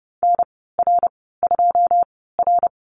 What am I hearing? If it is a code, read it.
Morse code, heard as a keyed tone, about 30 wpm, spaced as for 17 wpm, 700 Hz. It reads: NR2R
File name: Lesson 14g - Number 2 - Call Signs 30wpm